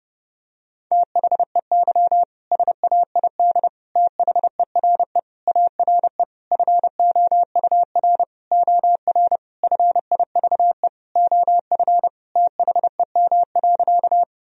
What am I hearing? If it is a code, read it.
Morse code, heard as a keyed tone, about 30 wpm, 700 Hz. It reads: THEY SAID THERE ARE FOUR OR FIVE OF THEM.
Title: Sentences from Top 1000 Words - Review Part 2 - 30wpm